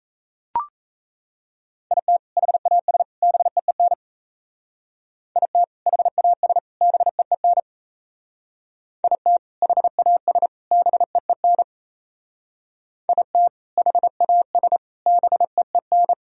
Morse code, heard as a keyed tone, about 33 wpm, 700 Hz. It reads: IT HAS BEEN IT HAS BEEN IT HAS BEEN IT HAS BEEN
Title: Speed-Racing - Sets of 3 Words 28wpm